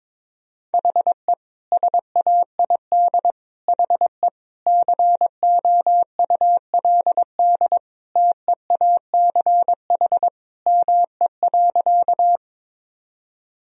Morse code, heard as a keyed tone, about 22 wpm, 700 Hz. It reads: HE SAID HE COULD TEACH ME.
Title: Sentences from Top 1000 Words - Review Part 2 - 22wpm